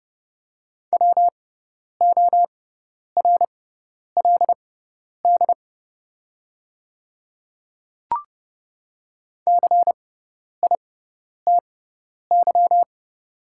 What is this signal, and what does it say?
Morse code, heard as a keyed tone, about 30 wpm, spaced as for 10 wpm, 690 Hz. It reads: WORLD CITY
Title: 6x Character Spacing - Top 300 Words - 30wpm